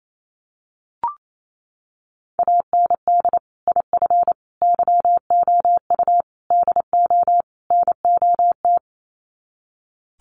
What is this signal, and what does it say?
Morse code, heard as a keyed tone, about 28 wpm, 700 Hz. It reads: AND IF YOU DO NOT
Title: Sets of 5 Words 28wpm